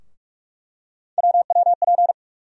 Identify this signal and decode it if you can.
Morse code, heard as a keyed tone, about 46 wpm, 700 Hz. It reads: WWP
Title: USA Call Sign Postfixes - 3 Letters 45wpm